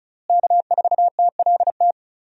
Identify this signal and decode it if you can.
Morse code, heard as a keyed tone, about 36 wpm, 700 Hz. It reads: K4TLT
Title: USA Call Signs - 5 Characters - Repeated 3x - 35wpm